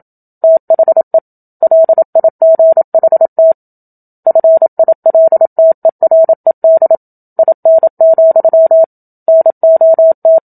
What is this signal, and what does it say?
Morse code, heard as a keyed tone, about 27 wpm, 650 Hz. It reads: THE LIGHT FILTERED IN, NOT
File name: Book Two / 01-Chapter 1 - 27 WPM